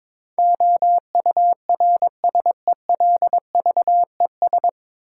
Morse code, heard as a keyed tone, about 22 wpm, 700 Hz. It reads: OURSELVES
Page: Morse Code Ninja practice 22wpm